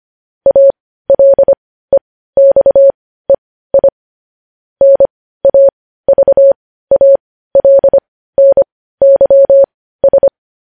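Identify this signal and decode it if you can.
Morse code, heard as a keyed tone, about 25 wpm, spaced as for 15 wpm, 550 Hz. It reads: ALEXEI NAVALNYS
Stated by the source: Morse news headlines practice at 15.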